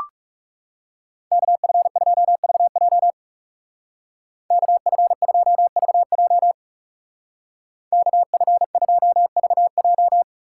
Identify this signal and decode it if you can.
Morse code, heard as a keyed tone, about 37 wpm, 700 Hz. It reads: KF2VJ KF2VJ KF2VJ
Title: Speed Racing - USA 2x2 Call Signs 30wpm